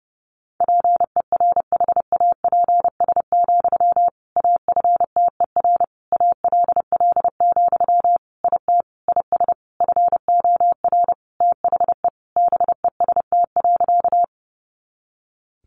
Morse code, heard as a keyed tone, about 30 wpm, 700 Hz. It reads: PERHAPS, AFTER ALL, IT IS FOR THE BEST.
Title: Sentences from Top 500 Words 30wpm